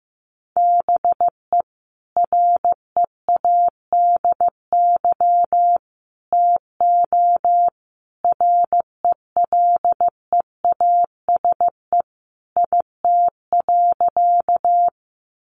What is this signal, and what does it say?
Morse code, heard as a keyed tone, about 15 wpm, 700 Hz. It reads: BE READY TO RELEASE IT.
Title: Sentences from Top 1100 Words 15wpm